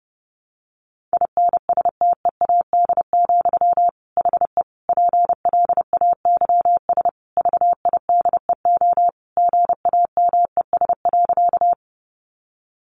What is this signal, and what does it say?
Morse code, heard as a keyed tone, about 30 wpm, 700 Hz. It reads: INSTEAD, HE PLAYS VIDEO GAMES.
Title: Additional Easy Phrases 30wpm